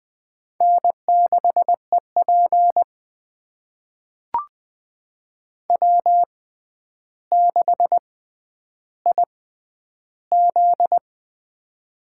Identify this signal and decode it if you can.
Morse code, heard as a keyed tone, about 20 wpm, spaced as for 10 wpm, 700 Hz. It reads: N6EP W6IZ
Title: USA Call Signs - 4 Characters - 6x Character Spacing - 20wpm